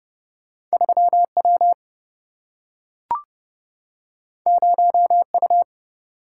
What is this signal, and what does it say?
Morse code, heard as a keyed tone, about 30 wpm, 700 Hz. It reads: 3 W 0 U
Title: Single-Letter Single-Number 30wpm